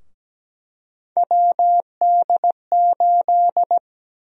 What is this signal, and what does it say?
Morse code, heard as a keyed tone, about 17 wpm, 700 Hz. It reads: WD8
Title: USA Call Sign Prefixes - Repeated 2x - 17wpm